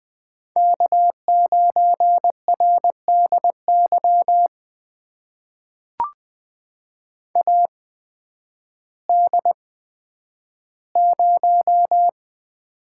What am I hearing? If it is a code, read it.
Morse code, heard as a keyed tone, about 20 wpm, spaced as for 11 wpm, 700 Hz. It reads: K9RDY AD0
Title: USA Call Signs - 8x Character Spacing - 20wpm